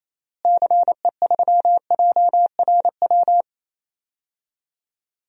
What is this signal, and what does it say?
Morse code, heard as a keyed tone, about 28 wpm, 700 Hz. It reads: CE3JRW
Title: DXCC-100 Call Signs - 6 Characters - Repeated 2x - 28wpm